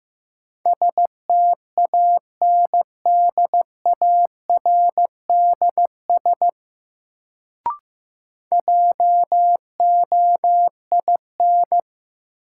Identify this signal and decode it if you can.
Morse code, heard as a keyed tone, about 15 wpm, 700 Hz. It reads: STANDARDS JOIN
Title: Sentences from Top 1000 Words - Review Part 6 - New Word Drill - 15wpm